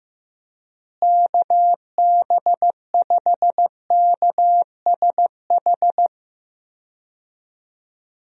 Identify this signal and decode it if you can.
Morse code, heard as a keyed tone, about 15 wpm, 700 Hz. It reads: KB5KSH
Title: USA Call Signs - 6 Characters - Repeated 2x - 15wpm